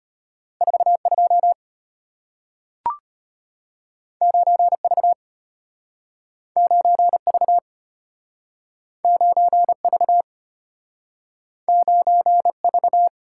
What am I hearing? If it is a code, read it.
Morse code, heard as a keyed tone, about 34 wpm, 700 Hz. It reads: 42 9V 9V 9V 9V